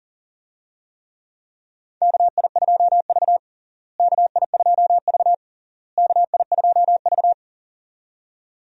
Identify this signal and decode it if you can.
Morse code, heard as a keyed tone, about 37 wpm, 700 Hz. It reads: KI2V KI2V KI2V
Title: USA Call Signs - 4 Characters - Repeated 3x - 40wpm